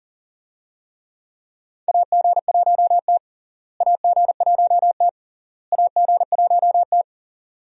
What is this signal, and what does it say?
Morse code, heard as a keyed tone, about 40 wpm, 700 Hz. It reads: AG1T AG1T AG1T
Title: USA Call Signs - 4 Characters - Repeated 3x - 40wpm